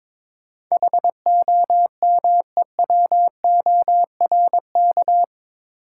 Morse code, heard as a keyed tone, about 22 wpm, 700 Hz. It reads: HOMEWORK
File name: practice / 22wpm / Eight Letter Words - 22wpm